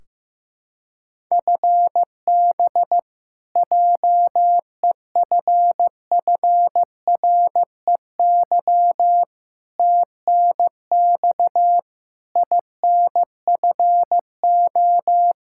Morse code, heard as a keyed tone, about 15 wpm, 700 Hz. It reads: FB JEFFREY TNX INFO
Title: K1USN SST QSO Practice 15wpm